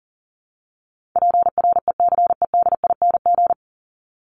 Morse code, heard as a keyed tone, about 40 wpm, 700 Hz. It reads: PRECEDING
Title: ING Suffix - Top 100 Words with Base Words 40wpm